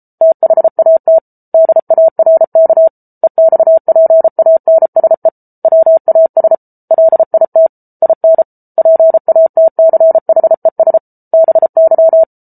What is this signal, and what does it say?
Morse code, heard as a keyed tone, about 34 wpm, 650 Hz. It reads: THAT DARK EXPANSE WAS LIT IN PATCHES BY